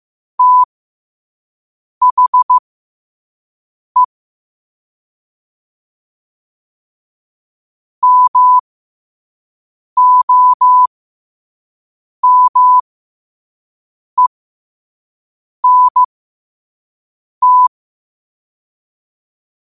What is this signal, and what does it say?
Morse code, heard as a keyed tone, about 15 wpm, spaced as for 5 wpm, 1000 Hz. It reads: THE MOMENT
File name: qotdfiles / 5wpm